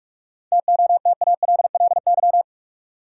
Morse code, heard as a keyed tone, about 46 wpm, 700 Hz. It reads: TOTALLY
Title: Sentences from Top 1600 Words - New Word Drill - 45wpm